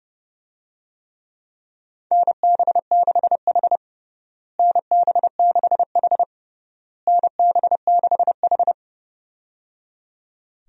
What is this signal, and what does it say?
Morse code, heard as a keyed tone, about 30 wpm, 700 Hz. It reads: NB6H NB6H NB6H
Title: USA Call Signs - 4 Characters - Repeated 3x - 30wpm